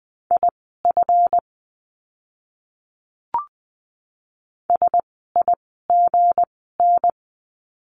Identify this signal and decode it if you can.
Morse code, heard as a keyed tone, about 20 wpm, spaced as for 14 wpm, 700 Hz. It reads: IF SIGN
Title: 160 Lesson 26f - Letter g - Words 20wpm